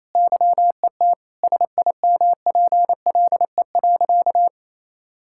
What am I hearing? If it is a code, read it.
Morse code, heard as a keyed tone, about 27 wpm, 700 Hz. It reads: YET SIMPLE.
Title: Sentences from Top 1000 Words - Review Part 6 - 28wpm